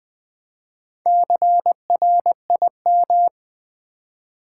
Morse code, heard as a keyed tone, about 20 wpm, 700 Hz. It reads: CRIM
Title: Build A Word - 4 Letters At A Time - 8-Letter Words - 20wpm